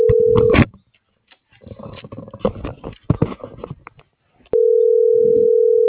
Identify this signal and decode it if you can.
Ringback tone (North America), 440 + 480 Hz.